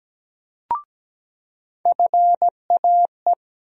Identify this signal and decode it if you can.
Morse code, heard as a keyed tone, about 17 wpm, 700 Hz. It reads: FAE
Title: USA Call Sign Postfixes - 3 Letters 17wpm